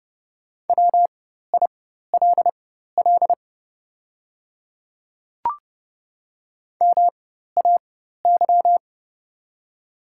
Morse code, heard as a keyed tone, about 30 wpm, spaced as for 14 wpm, 700 Hz. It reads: WILL MAY
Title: 4x Character Spacing - Top 300 Words - 30wpm